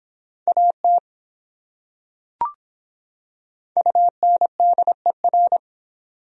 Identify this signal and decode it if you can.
Morse code, heard as a keyed tone, about 26 wpm, 700 Hz. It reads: AT UNDER